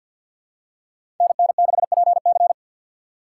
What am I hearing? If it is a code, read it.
Morse code, heard as a keyed tone, about 52 wpm, 700 Hz. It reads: NN6PC
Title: USA Call Signs - 5 Characters - 2x Character Spacing - 50wpm